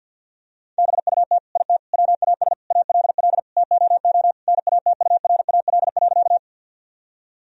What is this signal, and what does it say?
Morse code, heard as a keyed tone, about 50 wpm, 700 Hz. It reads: BUT IT WAS ALL TOO NATURAL.
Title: Sentences from Top 1000 Words - Review Part 4 - 50wpm